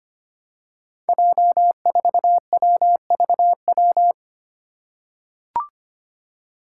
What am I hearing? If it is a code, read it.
Morse code, heard as a keyed tone, about 25 wpm, 700 Hz. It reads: J4WVW
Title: Five Challenging Letter-Number Combinations 25wpm